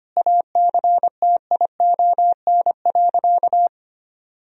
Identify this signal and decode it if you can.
Morse code, heard as a keyed tone, about 25 wpm, 700 Hz. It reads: ACTION.
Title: Sentences from Top 1000 Words - Review Part 4 - 25wpm